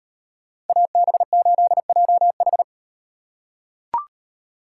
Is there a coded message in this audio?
Yes: Morse code from 0.7 to 2.6 s, heard as a keyed tone, about 36 wpm, 700 Hz. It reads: AB8JH